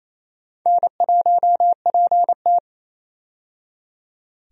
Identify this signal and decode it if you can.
Morse code, heard as a keyed tone, about 27 wpm, 700 Hz. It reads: N1PT
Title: USA Call Signs - 4 Characters - 28wpm